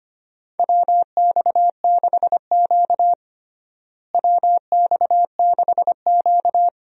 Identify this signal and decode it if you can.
Morse code, heard as a keyed tone, about 25 wpm, 700 Hz. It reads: WX6Q WX6Q
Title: USA Call Signs - 4 Characters - Repeated 2x - 25wpm